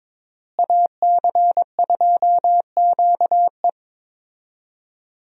Morse code, heard as a keyed tone, about 22 wpm, 700 Hz. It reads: AC2QE
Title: USA Call Signs - 5 Characters - Repeated 2x - 22wpm